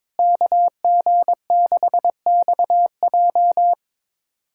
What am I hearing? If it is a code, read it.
Morse code, heard as a keyed tone, about 22 wpm, 700 Hz. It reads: KG6XJ
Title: USA Callsign DE Callsign - 5 Characters - 22wpm